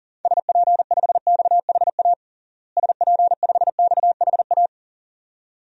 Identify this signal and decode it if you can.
Morse code, heard as a keyed tone, about 40 wpm, 700 Hz. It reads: SP5XHA SP5XHA
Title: DXCC-100 Call Signs - 6 Characters - Repeated 2x - 40wpm